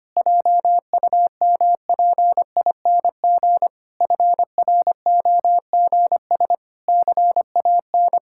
Morse code, heard as a keyed tone, about 25 wpm, 700 Hz. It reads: JUMPING FROGS CAN
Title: Pangram Challenge v2 - 25wpm